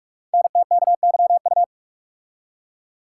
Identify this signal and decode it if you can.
Morse code, heard as a keyed tone, about 44 wpm, 700 Hz. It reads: NTKYU
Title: Code Groups - 5 Letters - 45wpm